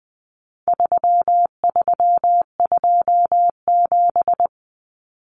Morse code, heard as a keyed tone, about 20 wpm, 700 Hz. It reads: 3327
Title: Common Cut Numbers - 4-Digit 20wpm